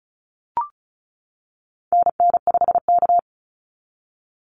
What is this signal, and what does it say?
Morse code, heard as a keyed tone, about 34 wpm, 700 Hz. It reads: NN5K
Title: NA SOTA Activator Callsigns 35wpm